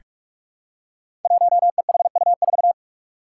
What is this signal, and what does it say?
Morse code, heard as a keyed tone, about 46 wpm, 700 Hz. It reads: 1EHU4
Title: Five Challenging Letter-Number Combinations 45wpm